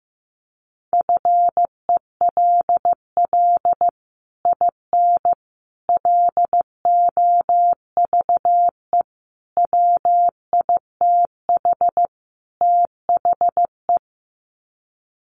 Morse code, heard as a keyed tone, about 15 wpm, 700 Hz. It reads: FELL IN LOVE WITH THE
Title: Sets of 5 Words Encore2 15wpm